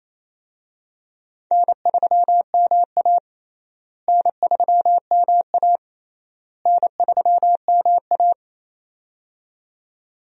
Morse code, heard as a keyed tone, about 28 wpm, 700 Hz. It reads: N3MA N3MA N3MA
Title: USA Call Signs - 4 Characters - Repeated 3x - 28wpm